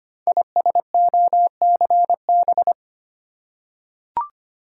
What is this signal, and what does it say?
Morse code, heard as a keyed tone, about 25 wpm, 700 Hz. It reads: ISOCB
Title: Code Groups - 5 Letters - 25wpm